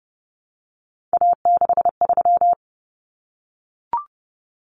Morse code, heard as a keyed tone, about 30 wpm, 700 Hz. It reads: A63
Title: Common Cut Numbers - 3-Digit 30wpm